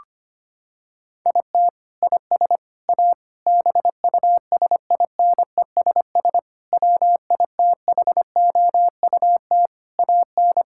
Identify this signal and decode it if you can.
Morse code, heard as a keyed tone, about 25 wpm, 700 Hz. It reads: IT IS A BUSINESS WITHOUT AN
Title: Sentences from Top 1600 Words 25wpm